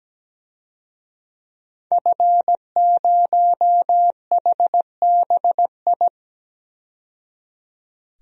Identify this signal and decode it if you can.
Morse code, heard as a keyed tone, about 17 wpm, 700 Hz. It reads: F0HBI
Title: DXCC-100 Call Signs - 5 Characters - 17wpm